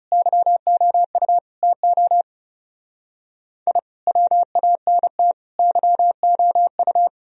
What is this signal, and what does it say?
Morse code, heard as a keyed tone, about 38 wpm, 700 Hz. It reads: YOU TO I WANT YOU